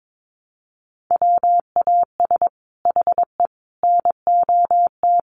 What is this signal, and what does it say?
Morse code, heard as a keyed tone, about 22 wpm, 700 Hz. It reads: WAS HE NOT
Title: Sentences from Top 1400 Words 22wpm